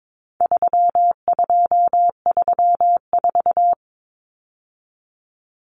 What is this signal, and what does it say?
Morse code, heard as a keyed tone, about 22 wpm, 700 Hz. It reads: 3234
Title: Common Cut Numbers - 4-Digit 22wpm